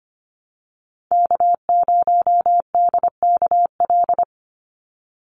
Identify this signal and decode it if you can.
Morse code, heard as a keyed tone, about 25 wpm, 700 Hz. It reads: K0DKL
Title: USA Call Signs 25wpm